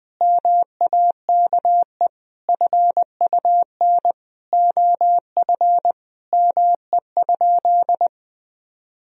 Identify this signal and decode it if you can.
Morse code, heard as a keyed tone, about 21 wpm, 700 Hz. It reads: MAKE FUN OF ME?
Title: Sentences from Top 1000 Words - Review Part 4 - 20wpm